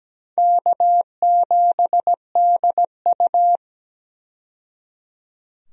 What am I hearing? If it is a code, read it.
Morse code, heard as a keyed tone, about 17 wpm, 700 Hz. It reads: K7DU
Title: USA Call Signs - 4 Characters - 17wpm